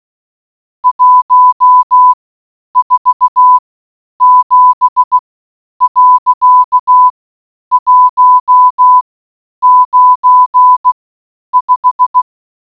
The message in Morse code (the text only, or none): 147.195